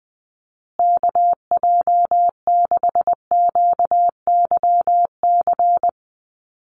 Morse code, heard as a keyed tone, about 20 wpm, 700 Hz. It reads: KJ6QYC
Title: USA Call Signs Repeated 2x 20wpm